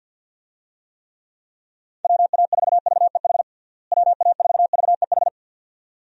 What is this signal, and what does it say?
Morse code, heard as a keyed tone, about 52 wpm, 700 Hz. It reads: WA4VEH WA4VEH
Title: USA Call Signs - 6 Characters - Repeated 2x - 50wpm